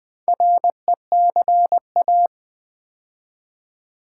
Morse code, heard as a keyed tone, about 20 wpm, 700 Hz. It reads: RECA
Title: Build A Word - 2 Letters At A Time - 6-Letter Words - 20wpm